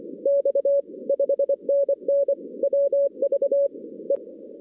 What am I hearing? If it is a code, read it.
Morse code, heard as a keyed tone, about 25 wpm, 560 Hz. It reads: X 5NN WV E
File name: 2016 arrl dx cw / 2016-ARRL-DX-CW-K5TR- - 0-46